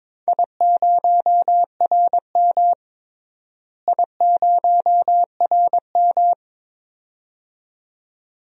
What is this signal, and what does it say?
Morse code, heard as a keyed tone, about 22 wpm, 700 Hz. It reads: I0RM I0RM
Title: DXCC-100 Call Signs - 4 Characters - Repeated 2x - 22wpm